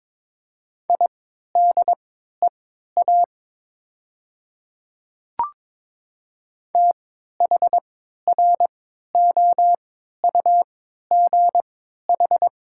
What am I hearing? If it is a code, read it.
Morse code, heard as a keyed tone, about 22 wpm, spaced as for 13 wpm, 700 Hz. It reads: IDEA THROUGH
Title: 3x Character Spacing - Top 300 Words - 22wpm